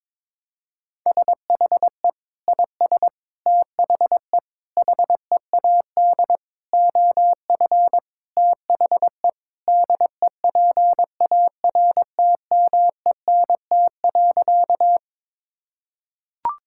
Morse code, heard as a keyed tone, about 21 wpm, 700 Hz. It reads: SHE IS THE HEAD OF THE DEPARTMENT.
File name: practice / 22wpm / Sentences with 4-Word Phrases - 22wpm